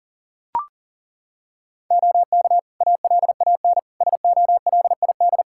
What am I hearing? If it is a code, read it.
Morse code, heard as a keyed tone, about 37 wpm, 700 Hz. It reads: OK ALAN SOLID
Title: Practice QSO - Exchange 3 - More Info - 40wpm